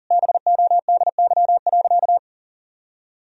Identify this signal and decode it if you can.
Morse code, heard as a keyed tone, about 40 wpm, 700 Hz. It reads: BODY.
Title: Sentences from Top 1000 Words - Review Part 2 - 40wpm